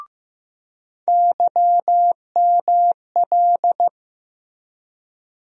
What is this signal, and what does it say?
Morse code, heard as a keyed tone, about 15 wpm, 700 Hz. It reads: YML
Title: USA Call Sign Postfixes - 3 Letters 15wpm